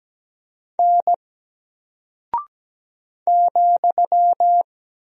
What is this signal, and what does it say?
Morse code, heard as a keyed tone, about 17 wpm, 700 Hz. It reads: N ,
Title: Single Character (Letter:Number:Punctuation:Prosign) Alt 17wpm